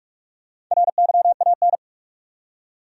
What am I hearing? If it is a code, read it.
Morse code, heard as a keyed tone, about 47 wpm, 700 Hz. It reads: RYAN
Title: USA Common Names - Max 5 Letters - 45wpm